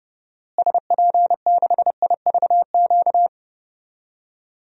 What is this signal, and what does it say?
Morse code, heard as a keyed tone, about 30 wpm, 700 Hz. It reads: SP6IVQ
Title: DXCC-100 Call Signs - 6 Characters - Repeated 2x - 30wpm